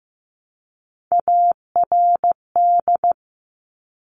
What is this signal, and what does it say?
Morse code, heard as a keyed tone, about 15 wpm, 700 Hz. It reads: ARD
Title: Top Three Letter Combinations in Words 15wpm